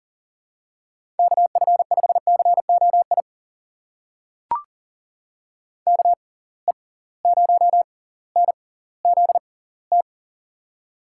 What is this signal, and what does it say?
Morse code, heard as a keyed tone, about 40 wpm, spaced as for 20 wpm, 700 Hz. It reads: KF5COI KE0NZT
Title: USA Call Signs - 6 Characters - 6x Character Spacing - 40wpm